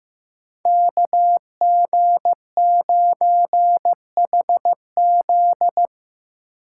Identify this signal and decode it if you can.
Morse code, heard as a keyed tone, about 15 wpm, 700 Hz. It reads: KG9HZ
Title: USA Call Signs - 5 Characters - 3x Character Spacing - 15wpm